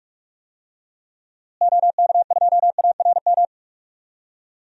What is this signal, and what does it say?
Morse code, heard as a keyed tone, about 42 wpm, 700 Hz. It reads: OK2ARM
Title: DXCC-100 Call Signs - 6 Characters - Repeated 3x - 45wpm